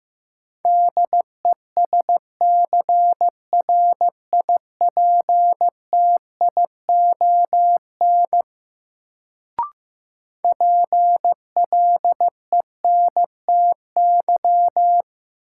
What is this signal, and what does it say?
Morse code, heard as a keyed tone, about 15 wpm, 700 Hz. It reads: DESCRIPTION PLENTY
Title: Sentences from Top 1300 Words - New Word Drill - 15wpm